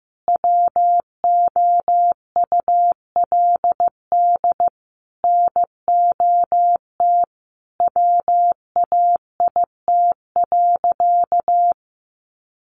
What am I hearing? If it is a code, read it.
Morse code, heard as a keyed tone, about 15 wpm, 700 Hz. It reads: WOULD NOT WAIT.
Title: Sentences from Top 1400 Words 15wpm